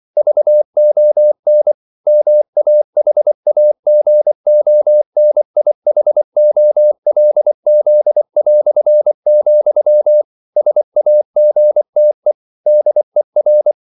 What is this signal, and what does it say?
Morse code, heard as a keyed tone, about 24 wpm, 600 Hz. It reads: VON MAHAGONIHOLZ", SAGTE DER